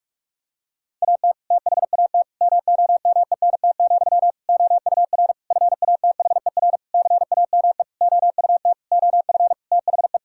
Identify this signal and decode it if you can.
Morse code, heard as a keyed tone, about 46 wpm, 700 Hz. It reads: AT THAT MOMENT, OUR FATHER CAME OUT OF THE
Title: Sentences from Top 1600 Words 45wpm